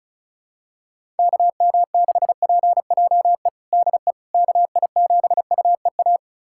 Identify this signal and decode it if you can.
Morse code, heard as a keyed tone, about 34 wpm, 700 Hz. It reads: KM6PJE DE KI7UEA
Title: USA Callsign DE Callsign 35wpm